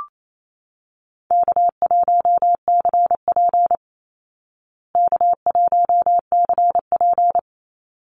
Morse code, heard as a keyed tone, about 28 wpm, 700 Hz. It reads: K1CP K1CP
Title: Easy Call Signs Repeated 2x - 28wpm